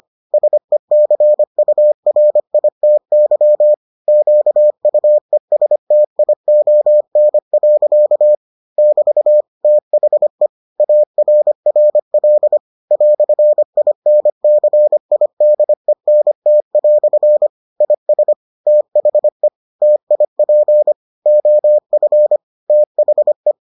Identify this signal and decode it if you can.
Morse code, heard as a keyed tone, about 25 wpm, 600 Hz. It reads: SECURITY QUESTION. <BT> THE ARRL "INCIDENT" IS THE TIP OF THE